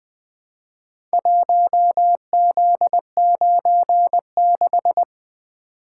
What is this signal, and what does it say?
Morse code, heard as a keyed tone, about 20 wpm, 700 Hz. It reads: 1Z96
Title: Four Letter-Number Combinations - 20wpm